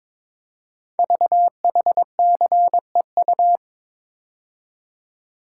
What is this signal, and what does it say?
Morse code, heard as a keyed tone, about 22 wpm, 700 Hz. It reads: VHCEU
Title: Code Groups - 5 Letters - 22wpm